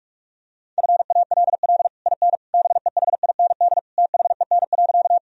Morse code, heard as a keyed tone, about 46 wpm, 700 Hz. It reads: FALL IN BEHIND THEN.
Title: Sentences from Top 1000 Words - Review Part 2 - 45wpm